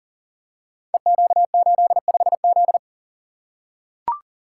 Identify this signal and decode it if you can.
Morse code, heard as a keyed tone, about 42 wpm, 700 Hz. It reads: EQ85Z
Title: Code Groups - 5 Letter:Numbers - 40wpm